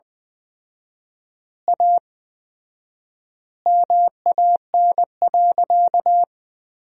Morse code, heard as a keyed tone, about 20 wpm, 700 Hz. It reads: A MAN.
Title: 4x Word-Spacing - Sentences from Top 300 Words - 20wpm